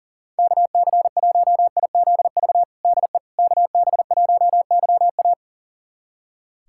Morse code, heard as a keyed tone, about 40 wpm, 700 Hz. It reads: KC1IZV DE KB1YA
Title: USA Callsign DE Callsign 40wpm